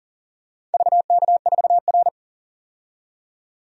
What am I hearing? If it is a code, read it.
Morse code, heard as a keyed tone, about 40 wpm, 700 Hz. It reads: VK4R
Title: DXCC-100 Call Signs - 4 Characters - 40wpm